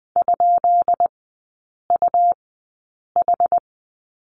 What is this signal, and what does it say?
Morse code, heard as a keyed tone, about 20 wpm, spaced as for 8 wpm, 690 Hz. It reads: ?UH
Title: 145 Lesson 24e - Letter y - Three Characters 20wpm